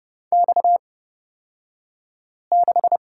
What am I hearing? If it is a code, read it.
Morse code, heard as a keyed tone, about 31 wpm, spaced as for 5 wpm, 700 Hz. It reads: X6
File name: Lesson 38d - Letter q - Two Characters 30wpm